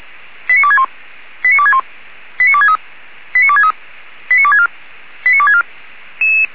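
Five-tone selective call (ZVEI), heard as quick runs of five tones, tones of about 70 ms. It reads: 89261 89261 89263 89263 89264 89264
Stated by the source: ZVEI five-tone selective call received on an SDR.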